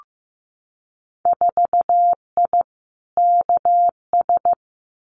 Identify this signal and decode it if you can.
Morse code, heard as a keyed tone, about 15 wpm, 700 Hz. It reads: 4I KS
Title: Winter Field Day Exchanges Encore 15wpm